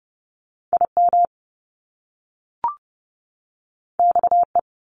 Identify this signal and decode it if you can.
Morse code, heard as a keyed tone, about 30 wpm, 700 Hz. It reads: IM XE
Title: Top Two Letter Combinations in Words 30wpm